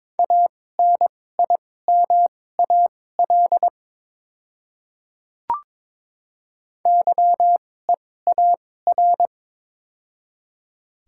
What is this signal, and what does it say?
Morse code, heard as a keyed tone, about 22 wpm, spaced as for 16 wpm, 700 Hz. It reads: ANIMAL YEAR
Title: 2x Character Spacing - Top 200 Words - 22wpm